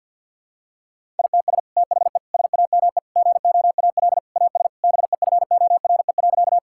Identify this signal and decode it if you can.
Morse code, heard as a keyed tone, about 50 wpm, 700 Hz. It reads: ITS THE SAME GOAL AS BEFORE.
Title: Sentences from Top 1000 Words - Review Part 5 - 50wpm